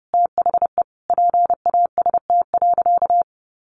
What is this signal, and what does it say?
Morse code, heard as a keyed tone, about 30 wpm, 700 Hz. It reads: THE PAST.
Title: A Sound Of Thunder 30wpm